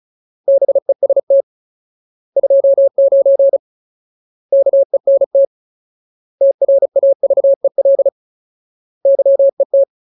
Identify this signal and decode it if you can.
Morse code, heard as a keyed tone, about 34 wpm, 550 Hz. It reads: BEST 29 KENT TRAVEL YET
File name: Ultimate Word List / 35WPM Ultimate